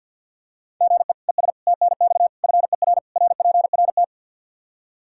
Morse code, heard as a keyed tone, about 48 wpm, 700 Hz. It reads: GE ES TNX FER RPRT
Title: Practice QSO - Exchange 1 - Beginning - 50wpm